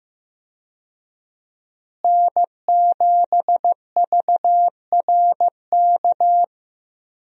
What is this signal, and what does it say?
Morse code, heard as a keyed tone, about 15 wpm, 700 Hz. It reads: N7VRK
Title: USA Call Signs - 5 Characters - Repeated 2x - 15wpm